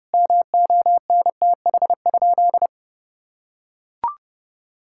Morse code, heard as a keyed tone, about 32 wpm, 700 Hz. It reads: MONTH?
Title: Sentences from Top 1000 Words - Review Part 6 - 30wpm